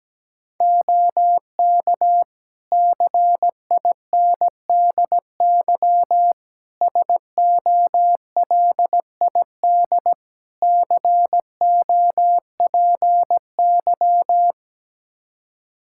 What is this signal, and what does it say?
Morse code, heard as a keyed tone, about 17 wpm, 700 Hz. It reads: OK CINDY SOLID COPY
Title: K1USN SST QSO Practice 17wpm